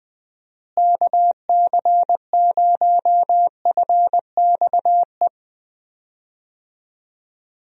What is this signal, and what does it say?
Morse code, heard as a keyed tone, about 20 wpm, 700 Hz. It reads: KC0FXE
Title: USA Call Signs - 6 Characters - Repeated 2x - 20wpm